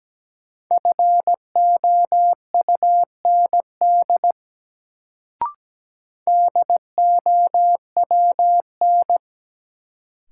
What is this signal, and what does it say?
Morse code, heard as a keyed tone, about 17 wpm, 700 Hz. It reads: FOUND DOWN
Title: Sentences from Top 300 Words - New Word Drill - 17wpm